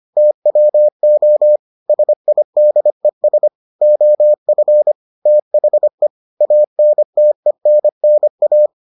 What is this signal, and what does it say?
Morse code, heard as a keyed tone, about 25 wpm, 600 Hz. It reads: TWO SIDES OF THE ANTENNA